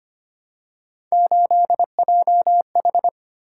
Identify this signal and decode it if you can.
Morse code, heard as a keyed tone, about 25 wpm, 700 Hz. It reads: 8JH
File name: Three Letter-Number Combinations - 25wpm